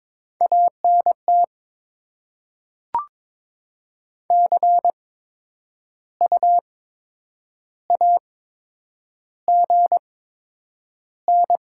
Morse code, heard as a keyed tone, about 22 wpm, spaced as for 9 wpm, 700 Hz. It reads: ANT CU AGN
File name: Top 20 QSO Elements - 8x Character Spacing - 22wpm